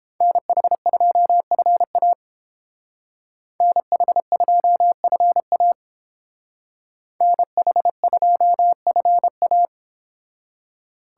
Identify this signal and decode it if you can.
Morse code, heard as a keyed tone, about 31 wpm, 700 Hz. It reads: NH2FA NH2FA NH2FA